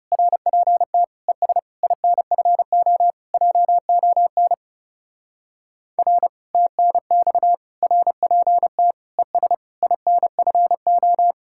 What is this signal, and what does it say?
Morse code, heard as a keyed tone, about 36 wpm, 700 Hz. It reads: RPT ES INFO JON R TNX RPT ES INFO
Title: Speed-Racing Common QSO Phrases - 30wpm